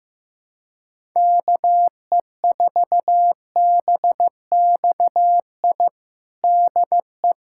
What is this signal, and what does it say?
Morse code, heard as a keyed tone, about 15 wpm, 700 Hz. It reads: KE4BXI DE
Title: USA Callsign DE Callsign 15wpm